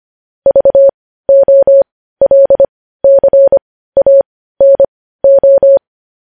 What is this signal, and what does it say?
Morse code, heard as a keyed tone, about 25 wpm, spaced as for 15 wpm, 550 Hz. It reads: VOLCANO
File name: podcast / News headlines-1x-25x15